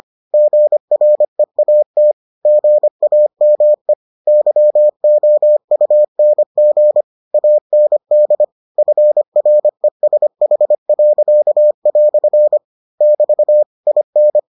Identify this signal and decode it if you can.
Morse code, heard as a keyed tone, about 25 wpm, 600 Hz. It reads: GREAT GAME YOUNG AND FRESH." <BT> IN